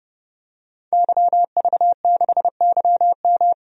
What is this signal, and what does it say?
Morse code, heard as a keyed tone, about 32 wpm, 700 Hz. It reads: YV6YM
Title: DXCC-100 Call Signs - 5 Characters - 30wpm